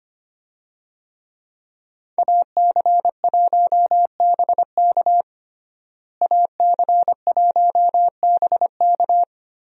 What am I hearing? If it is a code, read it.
Morse code, heard as a keyed tone, about 25 wpm, 700 Hz. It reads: AC1BK AC1BK
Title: USA Call Signs - 5 Characters - Repeated 3x - 25wpm